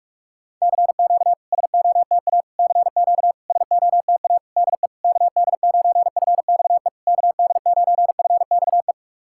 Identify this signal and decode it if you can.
Morse code, heard as a keyed tone, about 44 wpm, 690 Hz. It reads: CQ SOTA CQ SOTA DE KD9FXE KD9FXE
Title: SOTA QSO Practice 45wpm